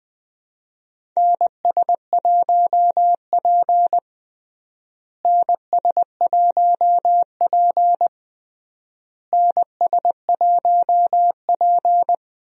Morse code, heard as a keyed tone, about 20 wpm, 700 Hz. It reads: NS1P NS1P NS1P
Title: USA Call Signs - 4 Characters - Repeated 3x - 20wpm